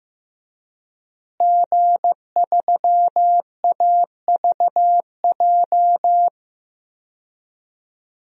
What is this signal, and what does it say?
Morse code, heard as a keyed tone, about 15 wpm, 700 Hz. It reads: G3AVJ
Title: DXCC-100 Call Signs - 5 Characters - Repeated 2x - 15wpm